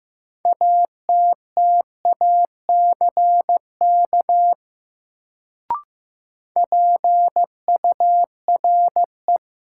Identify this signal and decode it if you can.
Morse code, heard as a keyed tone, about 15 wpm, 700 Hz. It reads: ATTACK PURE
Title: Sentences from Top 1500 Words - New Word Drill Alt - 15wpm